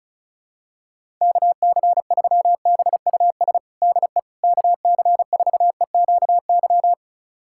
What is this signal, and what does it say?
Morse code, heard as a keyed tone, about 36 wpm, 700 Hz. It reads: KC3BUS DE KC4EQY
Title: USA Callsign DE Callsign 35wpm